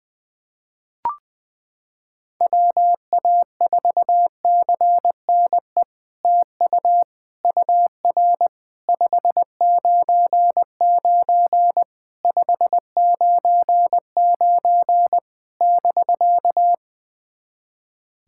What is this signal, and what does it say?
Morse code, heard as a keyed tone, about 20 wpm, 700 Hz. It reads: WA4CNE TU UR 599 599 <BK>
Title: SOTA QSO Practice 20wpm